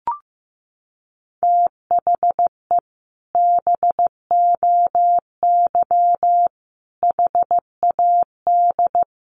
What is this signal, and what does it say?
Morse code, heard as a keyed tone, about 15 wpm, 700 Hz. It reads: THE BOY HAD
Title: Sentences from Top 1500 Words 15wpm